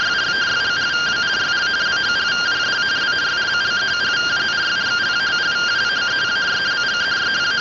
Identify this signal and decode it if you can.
Two-tone teleprinter signal, about 170 Hz shift.